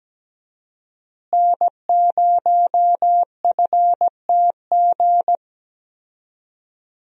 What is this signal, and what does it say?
Morse code, heard as a keyed tone, about 17 wpm, 700 Hz. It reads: N0FTG
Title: USA Call Signs - 5 Characters - Repeated 2x - 17wpm